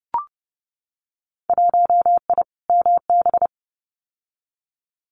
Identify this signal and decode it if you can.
Morse code, heard as a keyed tone, about 30 wpm, 700 Hz. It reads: 1I MB
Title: Winter Field Day Exchanges Encore 30wpm